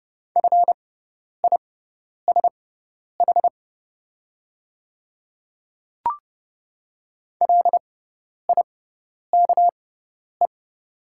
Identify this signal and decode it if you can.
Morse code, heard as a keyed tone, about 31 wpm, spaced as for 10 wpm, 700 Hz. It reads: FISH LIKE
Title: 6x Character Spacing - Top 300 Words - 30wpm